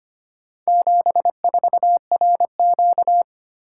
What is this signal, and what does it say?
Morse code, heard as a keyed tone, about 25 wpm, 700 Hz. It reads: 74RQ
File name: practice / 25wpm / Four Letter-Number Combinations - 25wpm